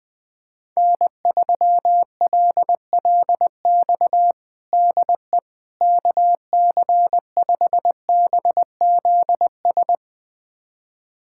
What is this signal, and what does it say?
Morse code, heard as a keyed tone, about 19 wpm, 700 Hz. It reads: N3LLX DE KC5BZS
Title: USA Callsign DE Callsign 20wpm